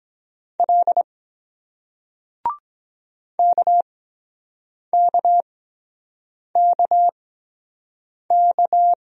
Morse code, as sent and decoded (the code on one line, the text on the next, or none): .-.. / -.- -.- -.- -.-
L KKKK